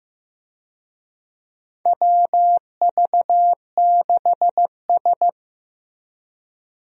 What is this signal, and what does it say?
Morse code, heard as a keyed tone, about 15 wpm, 700 Hz. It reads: WV6S
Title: USA Call Signs - 4 Characters - Repeated 3x - 15wpm